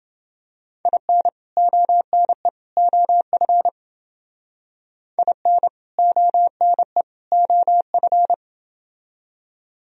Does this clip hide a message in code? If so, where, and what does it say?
Yes: Morse code from 0.9 to 8.3 s, heard as a keyed tone, about 28 wpm, 700 Hz. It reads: IN ONE OF IN ONE OF